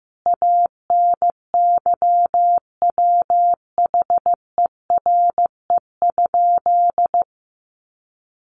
Morse code, heard as a keyed tone, about 15 wpm, 700 Hz. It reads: ANYWHERE?
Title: Sentences from Top 1400 Words 15wpm